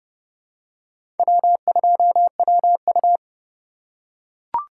Morse code, heard as a keyed tone, about 30 wpm, 700 Hz. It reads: W2WU
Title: USA Call Signs - 4x Character Spacing - 30wpm